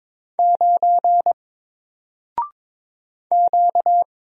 Morse code, heard as a keyed tone, about 22 wpm, 700 Hz. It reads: 9 Q
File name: Single-Letter-Number-22wpm